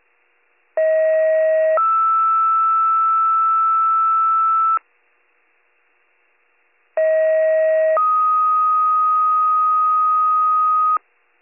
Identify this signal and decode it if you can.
Two-tone page, heard as a one-second tone, then a three-second tone, about 640 then 1320 Hz, about 640 then 1250 Hz. It is heard from 0.8 to 11.0 s.